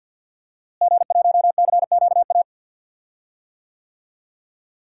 Morse code, heard as a keyed tone, about 50 wpm, 700 Hz. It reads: G1CQA
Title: DXCC-100 Call Signs - 5 Characters - Repeated 3x - 50wpm